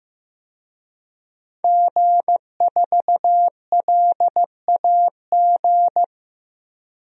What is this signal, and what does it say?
Morse code, heard as a keyed tone, about 15 wpm, 700 Hz. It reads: G4LAG
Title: DXCC-100 Call Signs - 5 Characters - 15wpm